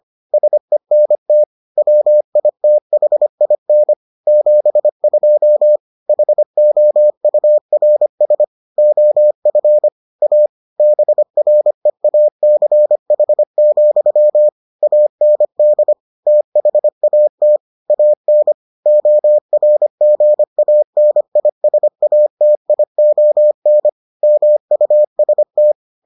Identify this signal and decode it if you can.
Morse code, heard as a keyed tone, about 25 wpm, 600 Hz. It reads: SENT WITHIN 72 HOURS OF A BREACH, AND THAT AN ORGANISATION MUST